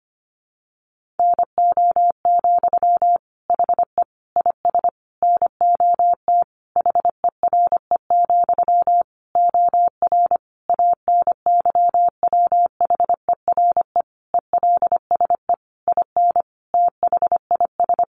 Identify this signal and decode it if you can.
Morse code, heard as a keyed tone, about 25 wpm, 700 Hz. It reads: NO, HE IS NOT HERE, OR ANYWHERE ELSE IN THIS
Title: Sentences from Top 1400 Words 25wpm